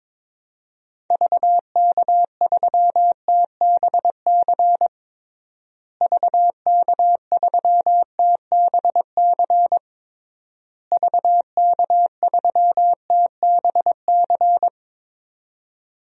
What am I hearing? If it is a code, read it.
Morse code, heard as a keyed tone, about 22 wpm, 700 Hz. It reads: VK3TBC VK3TBC VK3TBC
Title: DXCC-100 Call Signs - 6 Characters - Repeated 3x - 22wpm